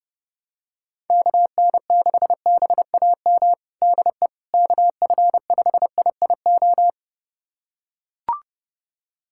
Morse code, heard as a keyed tone, about 30 wpm, 700 Hz. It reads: KN6BAM DE KF5IIO
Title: USA Callsign DE Callsign - 6 Characters - 30wpm